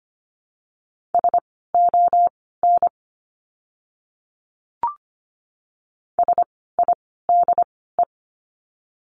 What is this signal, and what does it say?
Morse code, heard as a keyed tone, about 25 wpm, spaced as for 16 wpm, 700 Hz. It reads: SON SIDE
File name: course / 25:10wpm / 094 Lesson 16f - Letter c - Words 25wpm